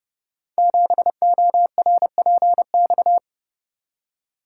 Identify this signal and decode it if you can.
Morse code, heard as a keyed tone, about 30 wpm, 700 Hz. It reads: 7ORPX
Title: Code Groups - 5 Letter:Numbers - 30wpm